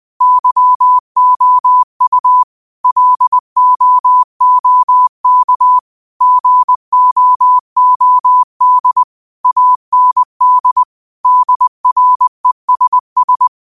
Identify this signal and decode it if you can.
Morse code, heard as a keyed tone, about 20 wpm, 1000 Hz. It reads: YOU LOOK GOOD AND DRESS